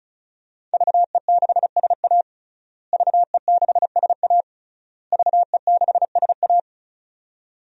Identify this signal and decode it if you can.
Morse code, heard as a keyed tone, about 34 wpm, 700 Hz. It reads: VE6SA VE6SA VE6SA
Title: DXCC-100 Call Signs - 5 Characters - Repeated 3x - 35wpm